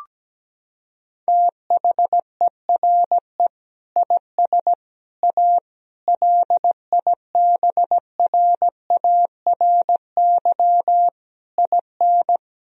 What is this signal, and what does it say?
Morse code, heard as a keyed tone, about 17 wpm, 700 Hz. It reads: THERE IS A LIBRARY IN
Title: Sentences from Top 1000 Words - Review Part 4 - 17wpm